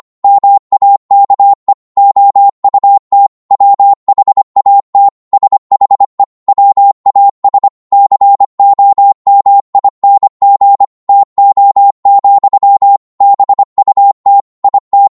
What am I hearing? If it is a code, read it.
Morse code, heard as a keyed tone, about 25 wpm, 800 Hz. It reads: MAKE OUT WHAT SHE WAS COMING TO, BUT IT